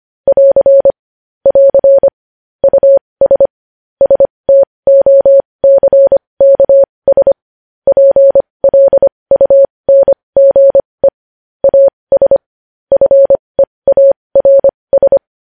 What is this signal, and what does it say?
Morse code, heard as a keyed tone, about 25 wpm, spaced as for 20 wpm, 550 Hz. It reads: <AR> <AR> US STOCKS PLUNGE AS FEARS
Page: News Headlines in Morse Code at 20 WPM